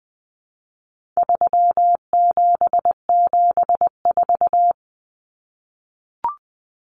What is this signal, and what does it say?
Morse code, heard as a keyed tone, about 20 wpm, 700 Hz. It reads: 3774
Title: Common Cut Numbers - 4-Digit 20wpm